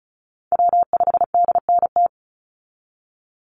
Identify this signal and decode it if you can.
Morse code, heard as a keyed tone, about 34 wpm, 700 Hz. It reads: W5DNT
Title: Call Signs Repeated 3x - 35wpm